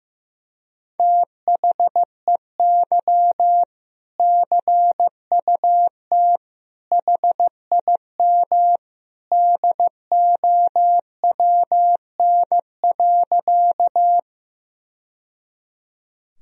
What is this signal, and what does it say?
Morse code, heard as a keyed tone, about 16 wpm, 700 Hz. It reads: THEY CUT HIM DOWN.
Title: Sentences from Top 1000 Words - Review Part 1 - 15wpm